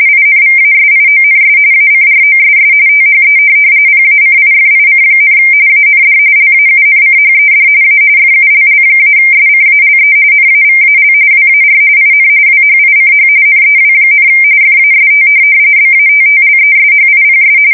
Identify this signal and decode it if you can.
RTTY, about 45.45 baud. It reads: INVASIONS ACROSS THE PACIFIC, AND LAUNCHED A FULL-SCALE NAVAL WAR IN WHICH NEW, FAST BATTLESHIPS
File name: fsk170 / USS-New-Jersey